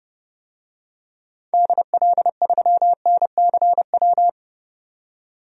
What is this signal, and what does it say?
Morse code, heard as a keyed tone, about 30 wpm, 700 Hz. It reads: DL3NCW
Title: DXCC-100 Call Signs - 6 Characters - 30wpm